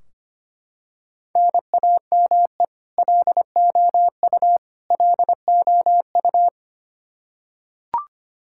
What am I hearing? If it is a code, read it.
Morse code, heard as a keyed tone, about 25 wpm, 700 Hz. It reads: NAME LOU LOU
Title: Practice QSO - Exchange 1 - Beginning - 25wpm